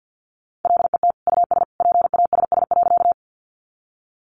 Morse code, heard as a keyed tone, about 50 wpm, 700 Hz. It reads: LET US PASS.
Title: Sentences from Top 400 Words 50wpm